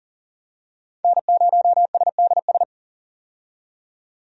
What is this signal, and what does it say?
Morse code, heard as a keyed tone, about 40 wpm, 700 Hz. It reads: N0SDS
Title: USA Call Signs - 5 Characters - 40wpm